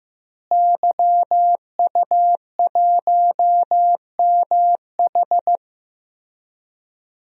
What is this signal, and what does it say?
Morse code, heard as a keyed tone, about 16 wpm, 700 Hz. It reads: YU1MH
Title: DXCC-100 Call Signs - 5 Characters - Repeated 3x - 15wpm